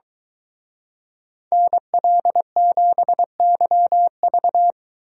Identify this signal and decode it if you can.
Morse code, heard as a keyed tone, about 23 wpm, 700 Hz. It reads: NL7YV